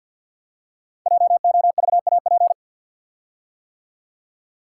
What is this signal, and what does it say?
Morse code, heard as a keyed tone, about 52 wpm, 700 Hz. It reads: JOVAP
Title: Code Groups - 5 Letters - 50wpm